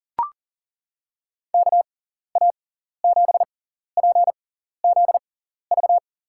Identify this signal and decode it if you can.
Morse code, heard as a keyed tone, about 39 wpm, spaced as for 14 wpm, 700 Hz. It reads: KA7PZV
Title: USA Call Signs - 6 Characters - 6x Character Spacing - 40wpm